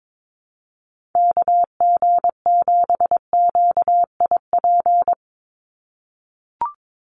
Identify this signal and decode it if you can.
Morse code, heard as a keyed tone, about 22 wpm, 700 Hz. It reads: KG7QIP
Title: USA Call Signs 22wpm